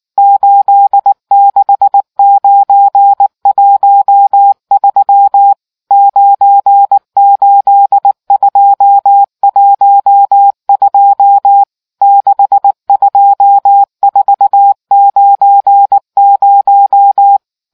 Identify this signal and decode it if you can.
Morse code, heard as a keyed tone, about 19 wpm, 800 Hz. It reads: 86913 98212 62490